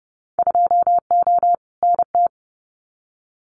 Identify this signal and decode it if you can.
Morse code, heard as a keyed tone, about 30 wpm, 700 Hz. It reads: 2O NT
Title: Winter Field Day Exchanges Encore 30wpm